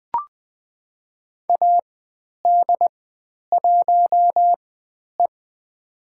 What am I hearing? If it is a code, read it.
Morse code, heard as a keyed tone, about 20 wpm, spaced as for 10 wpm, 700 Hz. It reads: AD1E
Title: 095 Lesson 16g - Letter c - Call Signs 20wpm